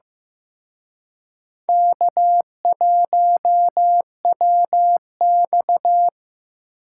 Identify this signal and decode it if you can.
Morse code, heard as a keyed tone, about 17 wpm, 700 Hz. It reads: K1WX
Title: Speed Racing - USA 1x2 Call Signs 15wpm